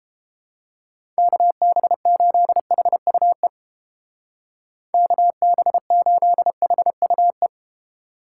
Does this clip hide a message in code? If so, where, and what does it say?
Yes: Morse code from 1.2 to 7.5 s, heard as a keyed tone, about 31 wpm, 700 Hz. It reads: KB8HUE KB8HUE